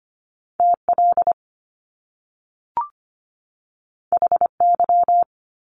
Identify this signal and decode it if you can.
Morse code, heard as a keyed tone, about 25 wpm, 700 Hz. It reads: TL HY
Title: Top Two Letter Combinations in Words 25wpm